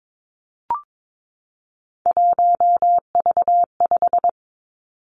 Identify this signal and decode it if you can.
Morse code, heard as a keyed tone, about 22 wpm, 700 Hz. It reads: 1V5
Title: Three Challenging Letter-Number Combinations 22wpm